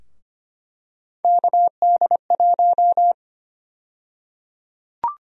Morse code, heard as a keyed tone, about 25 wpm, 700 Hz. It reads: KD1
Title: USA Call Sign Prefixes - 25wpm